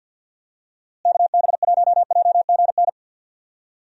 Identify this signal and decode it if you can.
Morse code, heard as a keyed tone, about 52 wpm, 700 Hz. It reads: KB1JGN
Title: USA Call Signs - 6 Characters - Repeated 3x - 50wpm